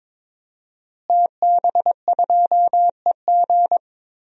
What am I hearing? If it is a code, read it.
Morse code, heard as a keyed tone, about 22 wpm, 700 Hz. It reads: TB2EG
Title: DXCC-100 Call Signs - 5 Characters - Repeated 3x - 22wpm